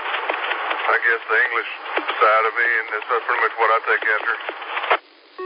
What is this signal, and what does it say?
Emergency Alert System (SAME) tones, heard as harsh buzzing bursts, with speech. It is heard from 1.0 to 4.4 s.